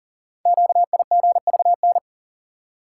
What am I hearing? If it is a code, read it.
Morse code, heard as a keyed tone, about 42 wpm, 700 Hz. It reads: QIGVN
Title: Code Groups - 5 Letters - 40wpm